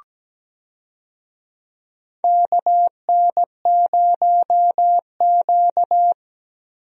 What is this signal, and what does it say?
Morse code, heard as a keyed tone, about 17 wpm, 700 Hz. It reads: KN0Q
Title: USA Call Signs - 4 Characters - Repeated 3x - 17wpm